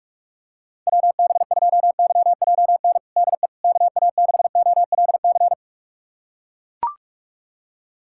Morse code, heard as a keyed tone, about 42 wpm, 700 Hz. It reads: WB2YJN DE KA6OLC
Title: USA Callsign DE Callsign - 6 Characters - 45wpm